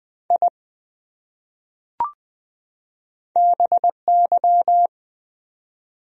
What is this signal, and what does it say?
Morse code, heard as a keyed tone, about 20 wpm, 700 Hz. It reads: I BY
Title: Top 100 Words 20wpm